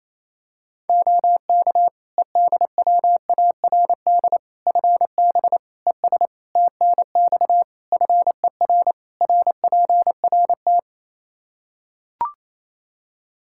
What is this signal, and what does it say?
Morse code, heard as a keyed tone, about 28 wpm, 700 Hz. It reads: OK EDWARD FB ES TNX FER RPRT
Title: Practice QSO - Exchange 2 - Info - 28wpm